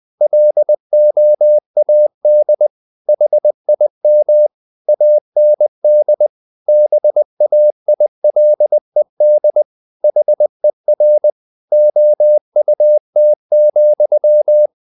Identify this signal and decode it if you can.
Morse code, heard as a keyed tone, about 20 wpm, 600 Hz. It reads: LOAD HIM AND BAILED HER OUT,
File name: the old man and the sea part 2 20wpm